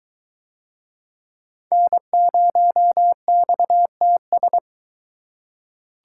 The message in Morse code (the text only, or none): N0XTS